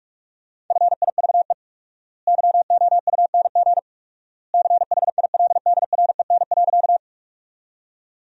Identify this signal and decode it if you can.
Morse code, heard as a keyed tone, about 46 wpm, 700 Hz. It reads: FIVE YOUNG CHILDREN.
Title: 4x Word-Spacing - Sentences from Top 400 Words - 45wpm